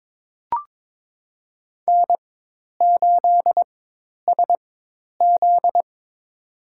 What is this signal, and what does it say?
Morse code, heard as a keyed tone, about 22 wpm, spaced as for 10 wpm, 700 Hz. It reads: N8SZ
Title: USA Call Signs - 4 Characters - 4x Character Spacing - 22wpm